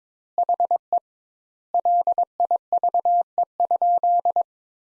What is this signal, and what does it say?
Morse code, heard as a keyed tone, about 22 wpm, 700 Hz. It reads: HE LIVE?
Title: Sentences Top 200 Words - 22wpm